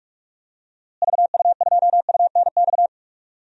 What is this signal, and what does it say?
Morse code, heard as a keyed tone, about 46 wpm, 700 Hz. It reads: VU2UNX
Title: DXCC-100 Call Signs - 6 Characters - Repeated 3x - 45wpm